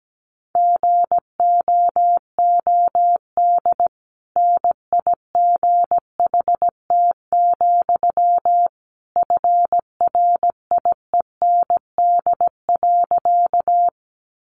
Morse code, heard as a keyed tone, about 17 wpm, 700 Hz. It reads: GOOD NIGHT, FRIEND.
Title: Sentences from Top 300 Words 17wpm